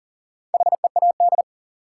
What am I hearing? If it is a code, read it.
Morse code, heard as a keyed tone, about 40 wpm, 700 Hz. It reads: HEAD
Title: Sets of 3 Words - New Word Drill - 40wpm